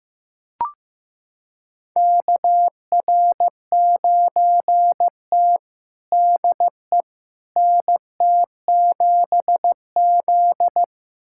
Morse code, heard as a keyed tone, about 16 wpm, 700 Hz. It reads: KR9T DE NT7Z
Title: USA Callsign DE Callsign - 4 Characters - 15wpm